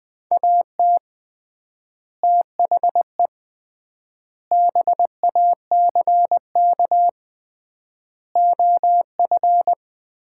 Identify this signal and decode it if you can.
Morse code, heard as a keyed tone, about 20 wpm, 700 Hz. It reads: AT THE BACK OF
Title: 3x Word-Spacing - Sets of 5 Words - 20wpm